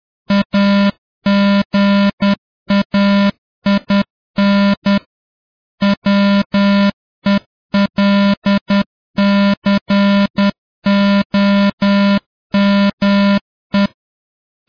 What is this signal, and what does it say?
Morse code, heard as a keyed tone, about 10 wpm, 200 Hz. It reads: AGAIN WELCOME